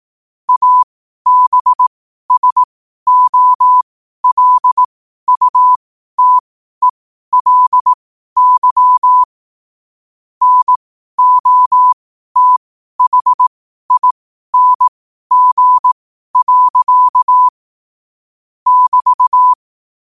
Morse code, heard as a keyed tone, about 18 wpm, spaced as for 12 wpm, 1000 Hz. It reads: ABSOLUTELY NOTHING. <BT>
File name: qotdfiles / 12wpm